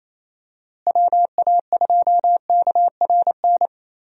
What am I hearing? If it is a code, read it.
Morse code, heard as a keyed tone, about 28 wpm, 700 Hz. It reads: WA2KRN
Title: USA Call Signs - 6 Characters - Repeated 2x - 28wpm